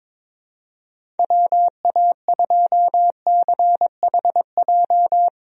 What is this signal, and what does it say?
Morse code, heard as a keyed tone, about 22 wpm, 700 Hz. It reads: WA2CHJ
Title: USA Call Signs - 6 Characters - 22wpm